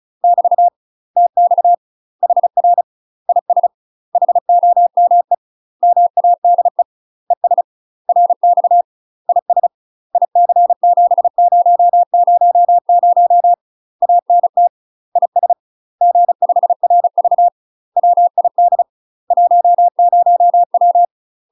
Morse code, heard as a keyed tone, about 34 wpm, 700 Hz. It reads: <BT> TX HR IS HOME MADE ES RX IS IC7000 ANT IS G5RV WID 10W
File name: UK MASTER QSOs TRAINING/35WPM